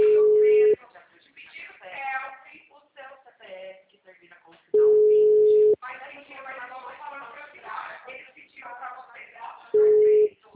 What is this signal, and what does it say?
Ringback tone (Europe, 425 Hz), with speech.
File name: out- -2007- - - .28221